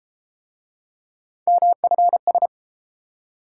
Morse code, heard as a keyed tone, about 31 wpm, 700 Hz. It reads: MFS